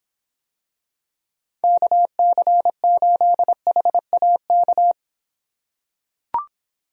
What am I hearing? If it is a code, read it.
Morse code, heard as a keyed tone, about 25 wpm, 700 Hz. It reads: KC8HAK